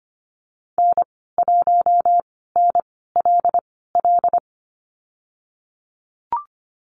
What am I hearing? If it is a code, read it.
Morse code, heard as a keyed tone, about 25 wpm, spaced as for 16 wpm, 700 Hz. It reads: N1NLL
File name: Lesson 15g - Number 5 - Call Signs 25wpm